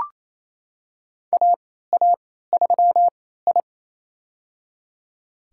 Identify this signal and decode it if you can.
Morse code, heard as a keyed tone, about 27 wpm, spaced as for 16 wpm, 700 Hz. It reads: AA3I
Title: USA Call Signs - 4 Characters - 3x Character Spacing - 28wpm